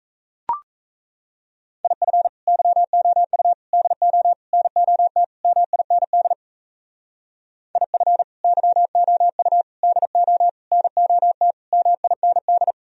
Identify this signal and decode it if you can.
Morse code, heard as a keyed tone, about 35 wpm, 700 Hz. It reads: IF YOU DO NOT MIND IF YOU DO NOT MIND
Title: Speed-Racing - Sets of 5 Words 28wpm